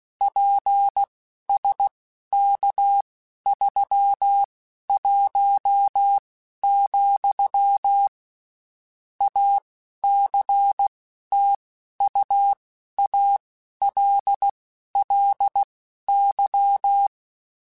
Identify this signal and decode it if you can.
Morse code, heard as a keyed tone, about 16 wpm, spaced as for 12 wpm, 800 Hz. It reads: PSK31, ACTUALLY
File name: (spaced) / QSOs Spaced-mp3s / kc7sac @15 sp